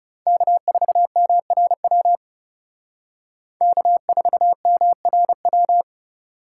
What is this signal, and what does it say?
Morse code, heard as a keyed tone, about 38 wpm, 700 Hz. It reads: K4MRW K4MRW